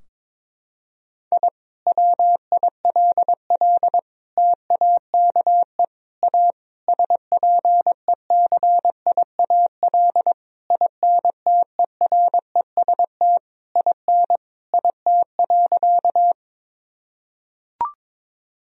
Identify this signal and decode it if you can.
Morse code, heard as a keyed tone, about 22 wpm, 700 Hz. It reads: I WILL TAKE A SPECIAL INTEREST IN IT.
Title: Sentences from Top 1000 Words - Review Part 3 - 22wpm